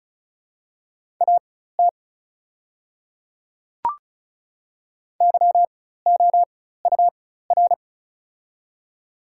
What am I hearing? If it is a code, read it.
Morse code, heard as a keyed tone, about 36 wpm, spaced as for 16 wpm, 700 Hz. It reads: AT YOUR
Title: 4x Character Spacing - Top 200 Words - 35wpm